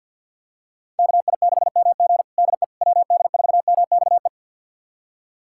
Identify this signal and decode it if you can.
Morse code, heard as a keyed tone, about 51 wpm, 700 Hz. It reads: KI6MG DE WD4MXE
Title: USA Callsign DE Callsign 50wpm